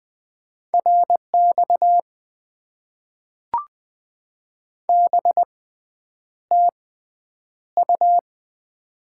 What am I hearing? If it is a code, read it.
Morse code, heard as a keyed tone, about 20 wpm, spaced as for 9 wpm, 700 Hz. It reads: RX BTU
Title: Top 90 QSO Elements - 6x Character Spacing - 20wpm